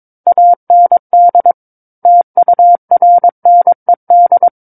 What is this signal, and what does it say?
Morse code, heard as a keyed tone, about 22 wpm, 700 Hz. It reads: AND TURNED
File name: Book One / 13-Chapter 13 - 22 WPM